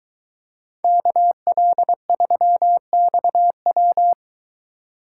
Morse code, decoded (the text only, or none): KL3XW